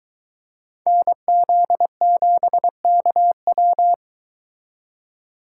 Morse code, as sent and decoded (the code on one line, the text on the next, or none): -. --.. --... -.- .--
NZ7KW